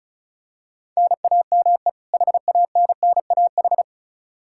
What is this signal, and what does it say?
Morse code, heard as a keyed tone, about 34 wpm, 700 Hz. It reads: NAME HANNAH
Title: Practice QSO - Exchange 1 - Beginning - 35wpm